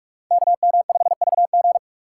Morse code, heard as a keyed tone, about 42 wpm, 700 Hz. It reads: KM5VG